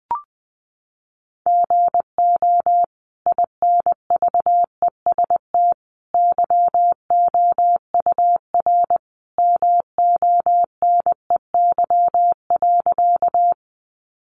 Morse code, heard as a keyed tone, about 20 wpm, 700 Hz. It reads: GO INVEST YOUR MONEY.
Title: Sentences from Top 1100 Words 20wpm